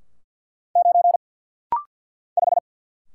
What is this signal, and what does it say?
Morse code, heard as a keyed tone, about 50 wpm, 700 Hz. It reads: 9 5
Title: Single Letter-Number - Rapid-Fire 50wpm